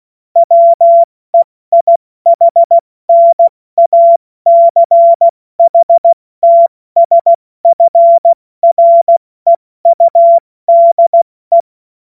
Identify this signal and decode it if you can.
Morse code, heard as a keyed tone, about 16 wpm, spaced as for 14 wpm, 680 Hz. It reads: WEIHNACHTSFREUDE
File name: Morsekurs / Witze / Witz 31. Januar 2023 70 F80